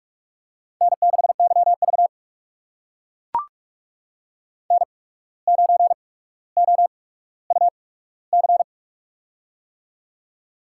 Morse code, heard as a keyed tone, about 44 wpm, spaced as for 18 wpm, 700 Hz. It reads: N6YV N9OUC
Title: USA Call Signs - 8x Character Spacing - 45wpm